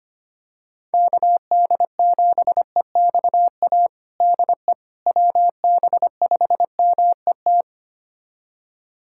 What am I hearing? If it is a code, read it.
Morse code, heard as a keyed tone, about 25 wpm, 700 Hz. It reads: KD7EXA DE WB5MET
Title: USA Callsign DE Callsign - 6 Characters - 25wpm